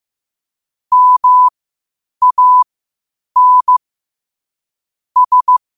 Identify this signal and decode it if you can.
Morse code, heard as a keyed tone, about 15 wpm, spaced as for 8 wpm, 1000 Hz. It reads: MAN S
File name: qotdfiles / 8wpm